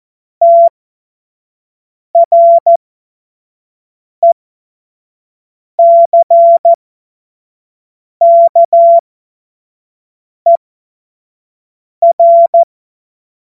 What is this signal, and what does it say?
Morse code, heard as a keyed tone, about 14 wpm, spaced as for 5 wpm, 680 Hz. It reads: TRECKER
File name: Witz-05-August-2025-25-F70